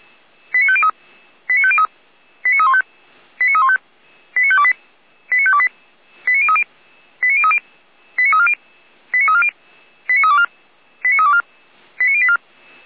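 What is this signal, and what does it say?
Five-tone selective call (ZVEI), heard as quick runs of five tones, tones of about 70 ms. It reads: 89503 89503 89315 89315 89528 89528 89030 89030 89340 89340 89324 89324 89084
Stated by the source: ZVEI five-tone call received by SDR.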